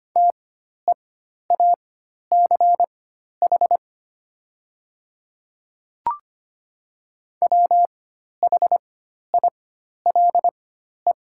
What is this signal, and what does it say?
Morse code, heard as a keyed tone, about 25 wpm, spaced as for 12 wpm, 700 Hz. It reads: TEACH WHILE
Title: 4x Character Spacing - Top 500 Words - 25wpm